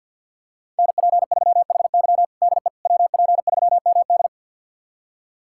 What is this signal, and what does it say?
Morse code, heard as a keyed tone, about 51 wpm, 700 Hz. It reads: NP3HY DE WP3MD
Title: USA Callsign DE Callsign - 5 Characters - 50wpm